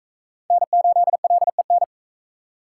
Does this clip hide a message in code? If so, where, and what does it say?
Yes: Morse code from 0.5 to 1.8 s, heard as a keyed tone, about 42 wpm, 700 Hz. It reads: N8LEN